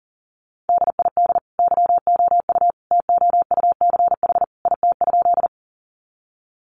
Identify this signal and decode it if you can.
Morse code, heard as a keyed tone, about 40 wpm, 700 Hz. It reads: DID YOU TOUCH IT?
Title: Sentences from Top 1200 Words 40wpm